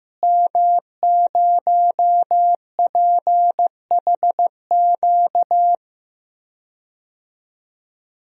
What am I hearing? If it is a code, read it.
Morse code, heard as a keyed tone, about 15 wpm, 700 Hz. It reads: M0PHQ
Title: DXCC-100 Call Signs - 5 Characters - Repeated 3x - 15wpm